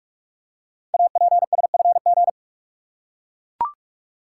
Morse code, heard as a keyed tone, about 44 wpm, 700 Hz. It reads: APSFG
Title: Code Groups - 5 Letters - 45wpm